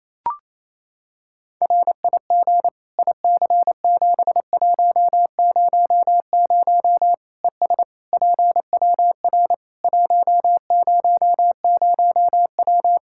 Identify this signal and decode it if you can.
Morse code, heard as a keyed tone, about 28 wpm, 700 Hz. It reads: RIG IC7100 ES PWR 100W
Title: Practice QSO - Exchange 2 - Info - 28wpm